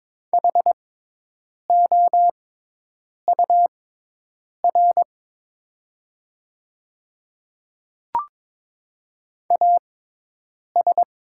Morse code, heard as a keyed tone, about 22 wpm, spaced as for 8 wpm, 700 Hz. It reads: HOUR AS
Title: 6x Character Spacing - Top 500 Words - 22wpm